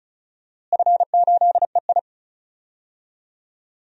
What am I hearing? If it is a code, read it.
Morse code, heard as a keyed tone, about 36 wpm, 700 Hz. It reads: F8EI
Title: DXCC-100 Call Signs - 4 Characters - Repeated 2x - 35wpm